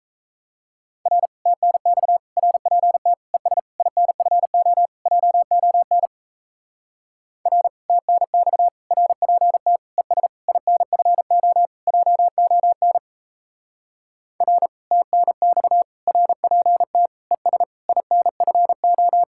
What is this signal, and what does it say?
Morse code, heard as a keyed tone, about 33 wpm, 700 Hz. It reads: R TNX RPT ES INFO JON R TNX RPT ES INFO JON R TNX RPT ES INFO
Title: Speed-Racing Common QSO Phrases Encore - 28wpm